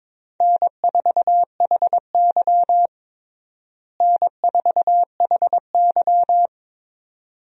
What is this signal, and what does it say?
Morse code, heard as a keyed tone, about 22 wpm, 700 Hz. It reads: N4HY N4HY
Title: USA Call Signs - 4 Characters - Repeated 2x - 22wpm